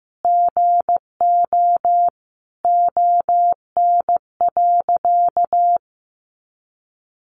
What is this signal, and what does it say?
Morse code, heard as a keyed tone, about 15 wpm, 700 Hz. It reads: GO ON.
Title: Sentences from Top 200 Words 15wpm